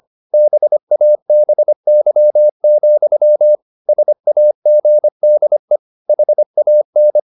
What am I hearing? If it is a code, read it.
Morse code, heard as a keyed tone, about 25 wpm, 600 Hz. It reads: BABY, SAGDE HAN
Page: Ferie 25 wpm